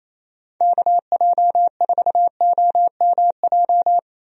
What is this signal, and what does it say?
Morse code, heard as a keyed tone, about 28 wpm, 700 Hz. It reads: KJ4OMJ
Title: USA Call Signs - 6 Characters - Repeated 2x - 28wpm